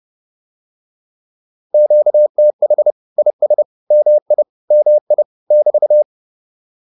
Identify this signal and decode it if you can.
Morse code, heard as a keyed tone, about 30 wpm, 600 Hz. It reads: QTH IS MI MI <BT>
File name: QSO108 30